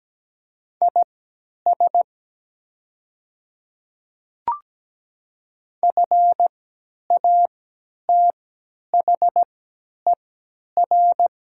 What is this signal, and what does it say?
Morse code, heard as a keyed tone, about 17 wpm, spaced as for 10 wpm, 700 Hz. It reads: IS FATHER
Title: 3x Character Spacing - Top 200 Words - 17wpm